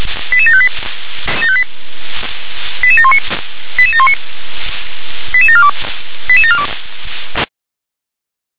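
Five-tone selective call (ZVEI), heard as quick runs of five tones, tones of about 70 ms. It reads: 88758 88719 88719 88742 88742
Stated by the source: ZVEI five-tone selective call received on an SDR.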